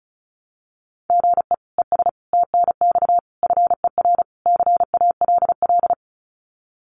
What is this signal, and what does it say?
Morse code, heard as a keyed tone, about 34 wpm, 700 Hz. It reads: GE ES TNX FER CALL
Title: Common QSO Phrases Encore 35wpm